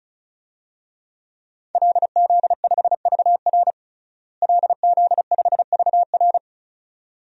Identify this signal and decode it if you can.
Morse code, heard as a keyed tone, about 34 wpm, 700 Hz. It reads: LZ5VR LZ5VR
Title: DXCC-100 Call Signs - 5 Characters - Repeated 2x - 35wpm